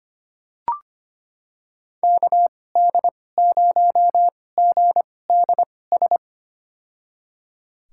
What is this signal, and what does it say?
Morse code, heard as a keyed tone, about 25 wpm, spaced as for 18 wpm, 700 Hz. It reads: KD0GDS
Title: USA Call Signs - 2x Character Spacing - 25wpm